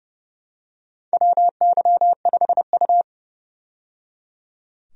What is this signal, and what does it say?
Morse code, heard as a keyed tone, about 30 wpm, 700 Hz. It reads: WY5U
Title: USA Call Signs - 4 Characters - 30wpm